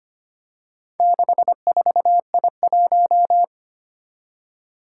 Morse code, heard as a keyed tone, about 25 wpm, 700 Hz. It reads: 64I1
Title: Four Challenging Letter-Number Combinations 25wpm